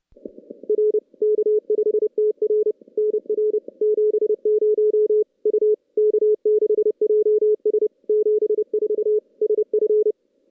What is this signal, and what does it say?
Morse code, heard as a keyed tone, about 27 wpm, 440 Hz. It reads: R K5TR NR 70 U K6JS 74 SF
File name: contests / CW-SS-k5tr- -2300